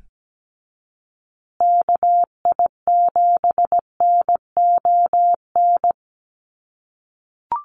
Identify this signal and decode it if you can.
Morse code, heard as a keyed tone, about 17 wpm, 700 Hz. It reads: KI7NON
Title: USA Call Signs Repeated 3x 17wpm